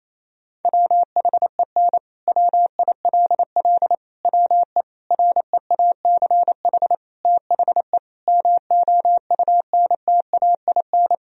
Morse code, heard as a keyed tone, about 28 wpm, 700 Hz. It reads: WHEN WILL WE REACH THE MOUNTAIN
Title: Sentences from Top 1000 Words - Review Part 4 - 28wpm